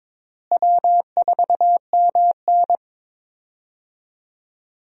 Morse code, heard as a keyed tone, about 22 wpm, 700 Hz. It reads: W4MN
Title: USA Call Signs - 4 Characters - Repeated 3x - 22wpm